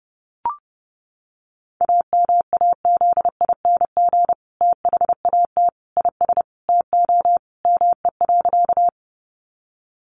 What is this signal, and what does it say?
Morse code, heard as a keyed tone, about 30 wpm, 700 Hz. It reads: AMAZING THAT IS TO ME.
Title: Sentences from Top 1500 Words 30wpm